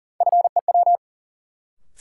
Morse code, heard as a keyed tone, about 40 wpm, 700 Hz. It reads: FEW
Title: Sentences from Top 300 Words - New Word Drill - 40wpm